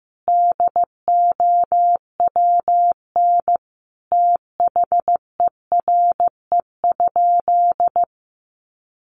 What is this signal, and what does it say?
Morse code, heard as a keyed tone, about 15 wpm, 700 Hz. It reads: DOWN THERE?
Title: Sentences from Top 1100 Words 15wpm